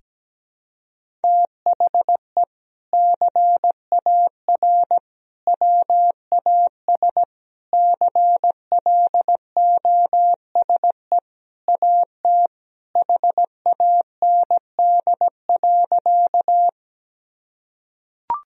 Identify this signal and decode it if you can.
Morse code, heard as a keyed tone, about 17 wpm, 700 Hz. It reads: THE CAR WAS CLOSE AT HAND.
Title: Sentences from Top 1000 Words - Review Part 1 - 17wpm